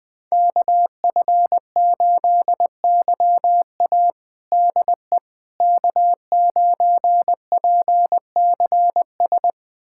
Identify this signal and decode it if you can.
Morse code, heard as a keyed tone, about 20 wpm, 700 Hz. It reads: KF8YA DE K9PCS
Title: USA Callsign DE Callsign - 5 Characters - 20wpm